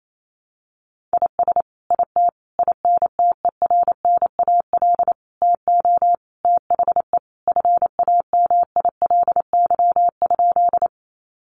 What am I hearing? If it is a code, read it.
Morse code, heard as a keyed tone, about 28 wpm, 700 Hz. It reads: IS IT INTERNAL TO THE FAMILY?
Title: Sentences from Top 1400 Words 28wpm